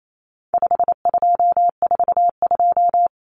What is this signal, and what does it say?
Morse code, heard as a keyed tone, about 28 wpm, 700 Hz. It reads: 5242
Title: Numbers - 4-Digit 28wpm